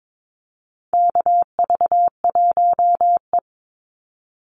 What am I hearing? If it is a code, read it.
Morse code, heard as a keyed tone, about 22 wpm, 700 Hz. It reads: KV1E
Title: Call Signs - 22wpm